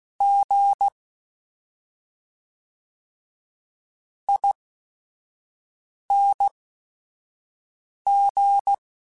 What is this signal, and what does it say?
Morse code, heard as a keyed tone, about 16 wpm, spaced as for 5 wpm, 800 Hz. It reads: G ING
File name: Special INGs / Special INGs-mp3s / SlowToFaster ing @5